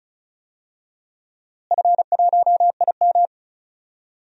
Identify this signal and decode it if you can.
Morse code, heard as a keyed tone, about 36 wpm, 700 Hz. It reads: F1IM
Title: DXCC-100 Call Signs - 4 Characters - Repeated 3x - 35wpm